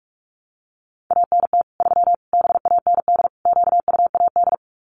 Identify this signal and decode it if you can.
Morse code, heard as a keyed tone, about 44 wpm, 700 Hz. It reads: ANT 3 BAND QUAD
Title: Common QSO Phrases 45wpm